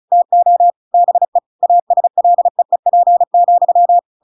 Morse code, heard as a keyed tone, about 34 wpm, 700 Hz. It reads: TO BE ASLEEP,
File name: Pinocchio / Pinocchio ch8 35 35 ik1hge